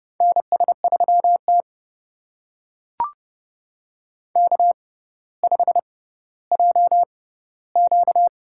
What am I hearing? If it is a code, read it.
Morse code, heard as a keyed tone, about 30 wpm, spaced as for 15 wpm, 700 Hz. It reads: NS3T K5JQ
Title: USA Call Signs - 4 Characters - 6x Character Spacing - 30wpm